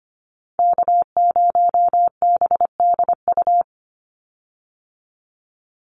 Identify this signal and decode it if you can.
Morse code, heard as a keyed tone, about 25 wpm, 700 Hz. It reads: K0BDU
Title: USA Call Signs 25wpm